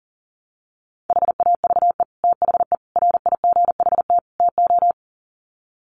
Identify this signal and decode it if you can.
Morse code, heard as a keyed tone, about 40 wpm, 700 Hz. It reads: HAVE THE RIGHT TO
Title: Sets of 4 Words 40wpm